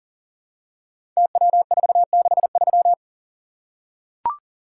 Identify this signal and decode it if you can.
Morse code, heard as a keyed tone, about 42 wpm, 700 Hz. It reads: TW463
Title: Code Groups - 5 Letter:Numbers - 40wpm